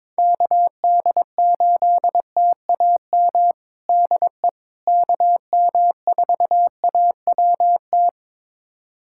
Morse code, heard as a keyed tone, about 21 wpm, 700 Hz. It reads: KD8TAM DE KM4AWT
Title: USA Callsign DE Callsign - 6 Characters - 22wpm